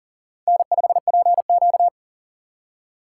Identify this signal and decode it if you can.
Morse code, heard as a keyed tone, about 40 wpm, 700 Hz. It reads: N5PQ
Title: USA Call Signs - 4 Characters - 2x Character Spacing - 40wpm